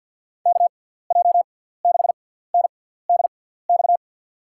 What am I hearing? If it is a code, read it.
Morse code, heard as a keyed tone, about 47 wpm, spaced as for 17 wpm, 700 Hz. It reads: KJ6NDX
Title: USA Call Signs - 6x Character Spacing - 50wpm